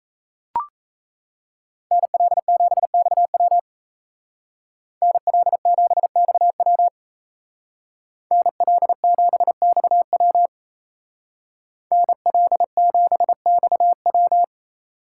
Morse code, heard as a keyed tone, about 36 wpm, 700 Hz. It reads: NL7XW NL7XW NL7XW NL7XW